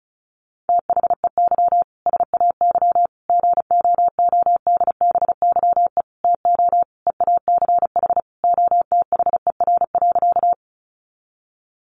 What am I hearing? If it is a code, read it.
Morse code, heard as a keyed tone, about 34 wpm, 700 Hz. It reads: THEY SAY GOODBYE TO EACH OTHER.
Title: Additional Easy Phrases 35wpm